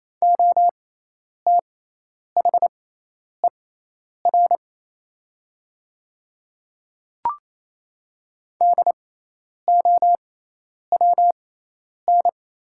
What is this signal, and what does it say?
Morse code, heard as a keyed tone, about 27 wpm, spaced as for 10 wpm, 690 Hz. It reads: OTHER DOWN
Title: 6x Character Spacing - Top 100 Words - 28wpm